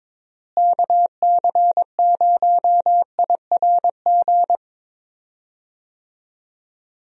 Morse code, heard as a keyed tone, about 22 wpm, 700 Hz. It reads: KC0IRG
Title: USA Call Signs - 6 Characters - Repeated 2x - 22wpm